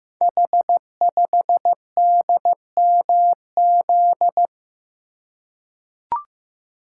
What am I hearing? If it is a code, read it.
Morse code, heard as a keyed tone, about 15 wpm, 700 Hz. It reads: H5DMZ
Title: Code Groups - 5 Letter:Numbers - 15wpm